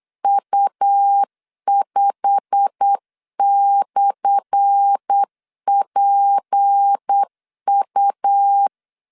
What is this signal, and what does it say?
Morse code, heard as a keyed tone, about 8 wpm, 800 Hz. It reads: U5/PU
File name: LEZIONI / 10 SIGNS / 10A SLASH 8wpm 3452